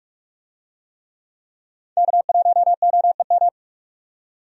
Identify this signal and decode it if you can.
Morse code, heard as a keyed tone, about 46 wpm, 700 Hz. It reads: K1OEM
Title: USA Call Signs - 5 Characters - 45wpm